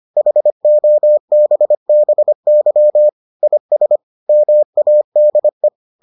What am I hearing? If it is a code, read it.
Morse code, heard as a keyed tone, about 25 wpm, 600 Hz. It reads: HOBBY IS MADE